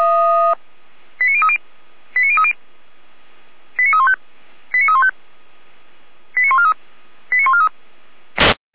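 Five-tone selective call (ZVEI), heard as quick runs of five tones, tones of about 70 ms. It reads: 89030 89030 89315 89315 89143 89143
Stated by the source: ZVEI five-tone selective call received by SDR, from feuerwehr-orlamuende fire service.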